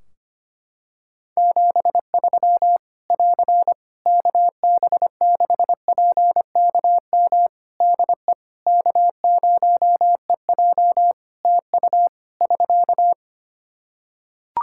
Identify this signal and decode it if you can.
Morse code, heard as a keyed tone, about 25 wpm, 700 Hz. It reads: 73 <AR> KB6PKM DE K0EJ TU <SK>
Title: Practice QSO - Exchange 4 - Ending - 25wpm